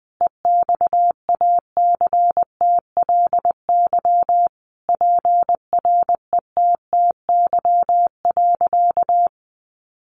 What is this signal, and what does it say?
Morse code, heard as a keyed tone, about 20 wpm, 700 Hz. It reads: EXACTLY PRETTY.
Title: Sentences from Top 1000 Words 20wpm